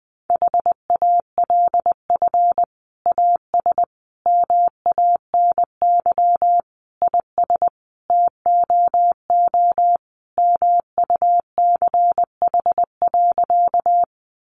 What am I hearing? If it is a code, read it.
Morse code, heard as a keyed tone, about 20 wpm, 700 Hz. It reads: HALF AS MANY IS TOO MUCH.
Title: Sentences from Top 400 Words 20wpm